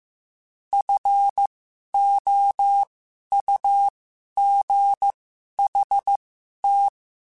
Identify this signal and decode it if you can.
Morse code, heard as a keyed tone, about 15 wpm, spaced as for 11 wpm, 800 Hz. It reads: FOUGHT
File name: F Runs / F Runs mp3s / F Runs @13